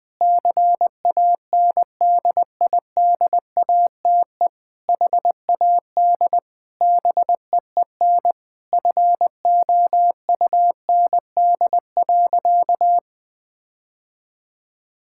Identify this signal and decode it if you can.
Morse code, heard as a keyed tone, about 19 wpm, 700 Hz. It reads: CANDIDATE HAD BEEN FOUND.
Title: Sentences from Top 1600 Words 20wpm